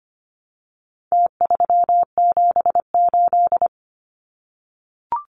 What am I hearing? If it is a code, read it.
Morse code, heard as a keyed tone, about 25 wpm, 700 Hz. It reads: T378
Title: Numbers - Contest Style - 4-Digit 25wpm